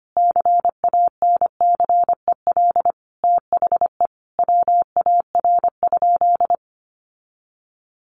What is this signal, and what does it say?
Morse code, heard as a keyed tone, about 25 wpm, 700 Hz. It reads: CANCEL THE WAR?
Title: Sentences from Top 1200 Words 25wpm